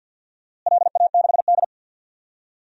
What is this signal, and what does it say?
Morse code, heard as a keyed tone, about 52 wpm, 700 Hz. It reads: LA6D
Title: DXCC-100 Call Signs - 4 Characters - Repeated 3x - 50wpm